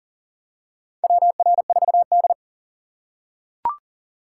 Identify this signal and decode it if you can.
Morse code, heard as a keyed tone, about 39 wpm, 700 Hz. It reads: WR4D
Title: USA Call Signs - 4 Characters - 3x Character Spacing - 40wpm